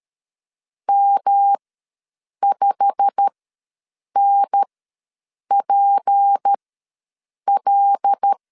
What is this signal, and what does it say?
Morse code, heard as a keyed tone, about 13 wpm, spaced as for 7 wpm, 800 Hz. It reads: M5NPL
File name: LEZIONI / 9 2867 ws 12wpm 3431